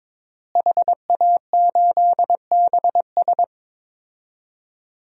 Morse code, heard as a keyed tone, about 22 wpm, 700 Hz. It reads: HA8BS
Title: DXCC-100 Call Signs - 5 Characters - Repeated 2x - 22wpm